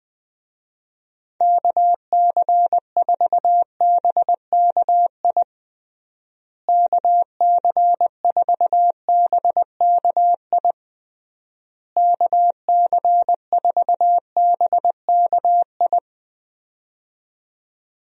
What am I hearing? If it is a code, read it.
Morse code, heard as a keyed tone, about 21 wpm, 700 Hz. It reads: KC4BKI KC4BKI KC4BKI
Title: USA Call Signs - 6 Characters - Repeated 3x - 20wpm